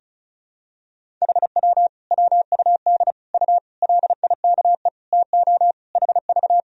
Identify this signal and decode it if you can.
Morse code, heard as a keyed tone, about 36 wpm, 700 Hz. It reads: HW WUD U LIKE TO HV
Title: Sentences from Top 100 Words and Abbreviations 35wpm